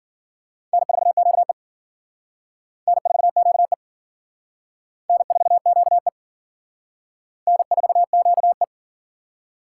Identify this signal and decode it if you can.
Morse code, heard as a keyed tone, about 51 wpm, 700 Hz. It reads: N4QE N4QE N4QE N4QE